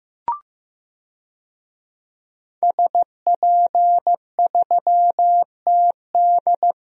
Morse code, heard as a keyed tone, about 15 wpm, 700 Hz. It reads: SP3TD
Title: DXCC-100 Call Signs - 5 Characters - Repeated 3x - 15wpm